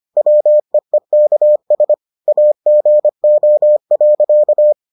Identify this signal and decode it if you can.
Morse code, heard as a keyed tone, about 25 wpm, 600 Hz. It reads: WEEKS AGO.